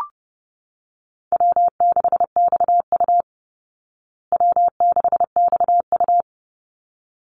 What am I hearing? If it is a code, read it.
Morse code, heard as a keyed tone, about 30 wpm, 700 Hz. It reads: W6XU W6XU
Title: Call Signs Repeated 3x - 30wpm